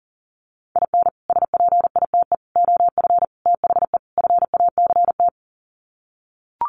Morse code, heard as a keyed tone, about 40 wpm, 700 Hz. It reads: IN SPITE OF THE FACT
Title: Sets of 5 Words Encore2 40wpm